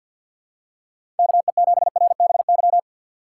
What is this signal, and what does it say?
Morse code, heard as a keyed tone, about 52 wpm, 700 Hz. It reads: KE7RBY
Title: USA Call Signs - 4x Character Spacing - 50wpm